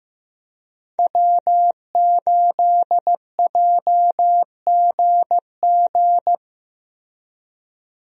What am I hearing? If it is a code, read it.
Morse code, heard as a keyed tone, about 15 wpm, 700 Hz. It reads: W8JGG
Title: USA Call Signs - 5 Characters - Repeated 3x - 15wpm